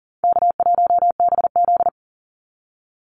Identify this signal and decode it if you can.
Morse code, heard as a keyed tone, about 40 wpm, 700 Hz. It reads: K1BZ
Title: Call Signs Repeated 2x - 40wpm